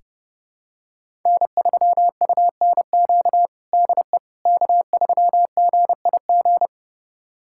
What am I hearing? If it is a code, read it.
Morse code, heard as a keyed tone, about 32 wpm, 700 Hz. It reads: N3UNQ DE K3GIG
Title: USA Callsign DE Callsign - 5 Characters - 30wpm